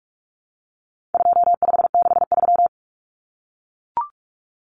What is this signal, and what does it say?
Morse code, heard as a keyed tone, about 44 wpm, 700 Hz. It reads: 2563
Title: Common Cut Numbers - 4-Digit 45wpm